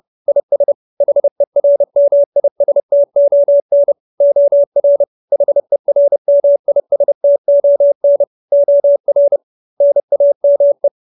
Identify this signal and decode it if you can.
Morse code, heard as a keyed tone, about 30 wpm, 570 Hz. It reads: IS HERMISTON OR HERMISTON OR NAME